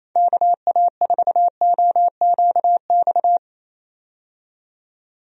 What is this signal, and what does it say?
Morse code, heard as a keyed tone, about 28 wpm, 700 Hz. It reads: KA4OQX
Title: USA Call Signs - 6 Characters - 28wpm